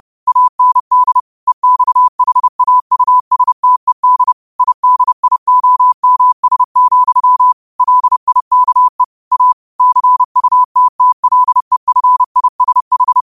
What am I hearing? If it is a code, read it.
Morse code, heard as a keyed tone, about 30 wpm, 1000 Hz. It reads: AND EXHAUSTED IDIOMS, LIKE A CUTTLEFISH
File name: qotdfiles / 30wpm